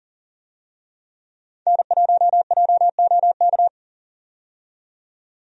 Morse code, heard as a keyed tone, about 40 wpm, 700 Hz. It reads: N1JOK
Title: USA Call Signs - 5 Characters - 40wpm